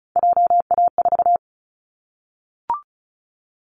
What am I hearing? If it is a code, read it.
Morse code, heard as a keyed tone, about 34 wpm, 700 Hz. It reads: JA4
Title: Callsign Prefixes for Top 25 DXCC - 35wpm